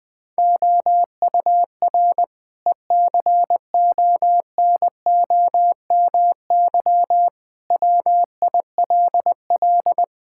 Morse code, heard as a keyed tone, about 20 wpm, 700 Hz. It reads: OUR ECONOMY WILL
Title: Sentences from Top 1000 Words - Review Part 5 - 20wpm